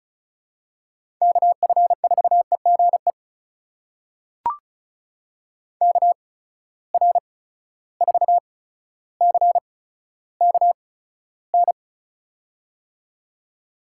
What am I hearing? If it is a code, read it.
Morse code, heard as a keyed tone, about 36 wpm, spaced as for 15 wpm, 700 Hz. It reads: KF4EGE KR4CKN
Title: USA Call Signs - 8x Character Spacing - 35wpm